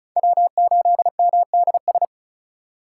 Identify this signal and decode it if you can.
Morse code, heard as a keyed tone, about 34 wpm, 700 Hz. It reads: W8MDS
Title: USA Call Signs - 6x Character Spacing - 35wpm